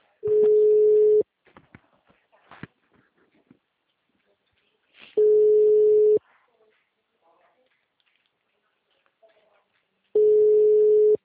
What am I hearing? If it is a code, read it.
Ringback tone (Europe, 425 Hz).